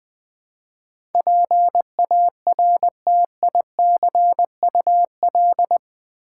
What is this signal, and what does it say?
Morse code, heard as a keyed tone, about 21 wpm, 700 Hz. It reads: PARTICUL
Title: Build A Word - 4 Letters At A Time - 12-Letter Words - 20wpm